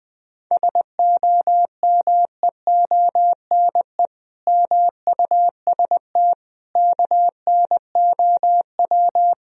Morse code, heard as a keyed tone, about 20 wpm, 700 Hz. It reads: SOMEONE MUST KNOW
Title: Sentences from Top 1000 Words - Review Part 4 - 20wpm